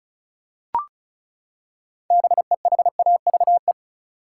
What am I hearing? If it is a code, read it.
Morse code, heard as a keyed tone, about 36 wpm, 700 Hz. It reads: BEHAVE
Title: Six Letter Words - 35wpm